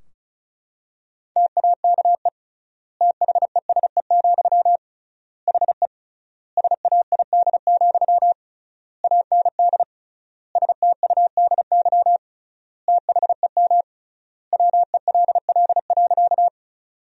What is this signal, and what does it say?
Morse code, heard as a keyed tone, about 36 wpm, 700 Hz. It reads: TAKE THESE, HE SAID, AND STUDY THEM WELL.
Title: 3x Word-Spacing - Sentences from Top 300 Words - 35wpm